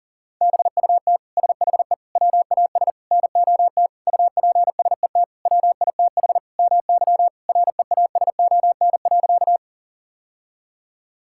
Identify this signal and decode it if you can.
Morse code, heard as a keyed tone, about 40 wpm, 700 Hz. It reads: BUT SHE WAS NOT UPSET WITH MY REASON.
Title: Sentences from Top 1000 Words - Review Part 5 - 40wpm